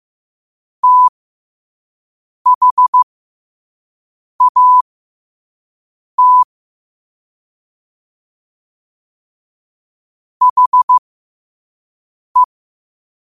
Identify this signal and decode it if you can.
Morse code, heard as a keyed tone, about 15 wpm, spaced as for 5 wpm, 1000 Hz. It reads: THAT HE
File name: qotdfiles / 5wpm